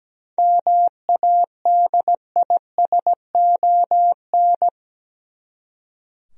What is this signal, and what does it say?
Morse code, heard as a keyed tone, about 17 wpm, 700 Hz. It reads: MADISON
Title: USA Common Names - 17wpm